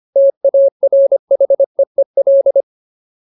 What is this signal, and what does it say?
Morse code, heard as a keyed tone, about 25 wpm, 550 Hz. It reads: TARHEEL
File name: Ultimate Word List / 25WPM Ultimate